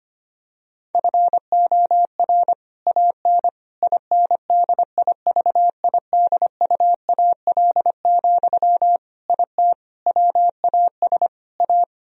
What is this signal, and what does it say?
Morse code, heard as a keyed tone, about 25 wpm, 700 Hz. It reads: FOR AN INDIVIDUAL, IT WAS A
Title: Sentences from Top 1000 Words - Review Part 4 - 25wpm